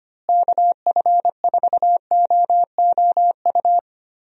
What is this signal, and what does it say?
Morse code, heard as a keyed tone, about 25 wpm, 700 Hz. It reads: KF4OOU
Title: USA Call Signs - 6 Characters - Repeated 2x - 25wpm